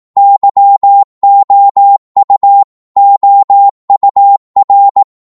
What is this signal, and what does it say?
Morse code, heard as a keyed tone, about 18 wpm, 800 Hz. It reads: YOU OUR
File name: Morse / psg-18wpm